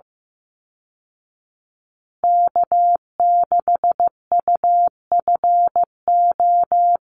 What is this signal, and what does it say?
Morse code, heard as a keyed tone, about 14 wpm, 700 Hz. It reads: K6UFO
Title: Call Signs Repeated 3x - 15wpm